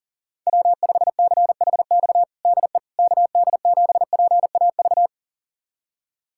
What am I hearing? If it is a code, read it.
Morse code, heard as a keyed tone, about 40 wpm, 700 Hz. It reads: W5CHX DE KD7PAV
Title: USA Callsign DE Callsign 40wpm